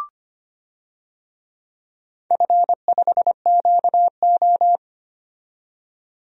Morse code, heard as a keyed tone, about 25 wpm, 700 Hz. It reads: F5QO
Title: DXCC-100 Call Signs - 4 Characters - 25wpm